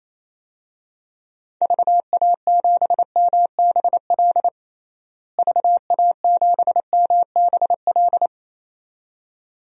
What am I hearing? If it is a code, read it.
Morse code, heard as a keyed tone, about 28 wpm, 700 Hz. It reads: VA7MBL VA7MBL
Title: DXCC-100 Call Signs - 6 Characters - Repeated 2x - 28wpm